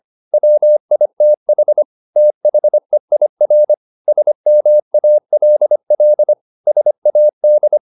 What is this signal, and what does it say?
Morse code, heard as a keyed tone, about 25 wpm, 600 Hz. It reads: WITH THEIR SMALL SAD
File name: the old man and the sea part 1 25wpm